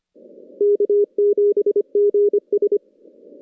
Morse code, heard as a keyed tone, about 29 wpm, 410 Hz. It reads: K7GS
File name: contests / CW-SS-k5tr- -2200